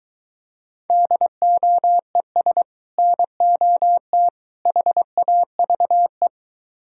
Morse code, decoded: DOES NOT HAVE

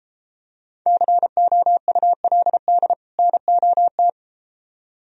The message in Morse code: COULD NOT

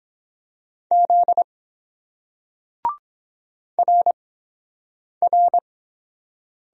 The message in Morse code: Z RR

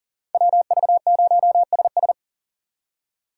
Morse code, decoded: WV0SS

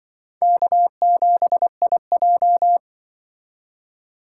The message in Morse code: K7IJ

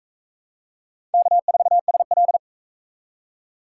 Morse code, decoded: K4SL

